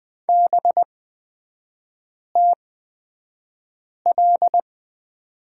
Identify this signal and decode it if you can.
Morse code, heard as a keyed tone, about 20 wpm, spaced as for 5 wpm, 700 Hz. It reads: BTL
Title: 224 Lesson 36e - BT - Three Characters 20wpm